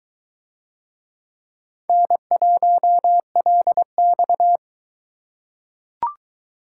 Morse code, heard as a keyed tone, about 22 wpm, 700 Hz. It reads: N1LX